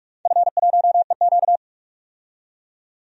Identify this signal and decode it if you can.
Morse code, heard as a keyed tone, about 46 wpm, 700 Hz. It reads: F1EQ
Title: DXCC-100 Call Signs - 4 Characters - 45wpm